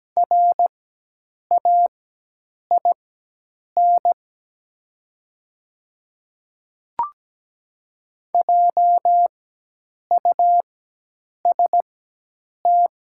Morse code, heard as a keyed tone, about 17 wpm, spaced as for 8 wpm, 700 Hz. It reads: RAIN JUST
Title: 4x Character Spacing - Top 300 Words - 17wpm